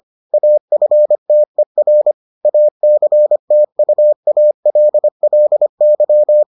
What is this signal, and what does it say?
Morse code, heard as a keyed tone, about 25 wpm, 600 Hz. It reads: AFTER ACTUALLY